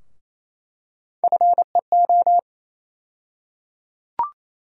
Morse code, heard as a keyed tone, about 28 wpm, 700 Hz. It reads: FEO
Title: USA Call Sign Postfixes - 3 Letters 28wpm